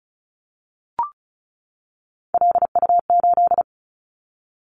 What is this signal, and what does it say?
Morse code, heard as a keyed tone, about 34 wpm, 700 Hz. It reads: LU8
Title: Callsign Prefixes for Top 50 DXCC - 35wpm